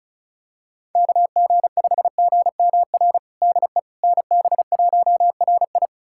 Morse code, heard as a keyed tone, about 36 wpm, 700 Hz. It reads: KG5GMR DE NB1RI
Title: USA Callsign DE Callsign 35wpm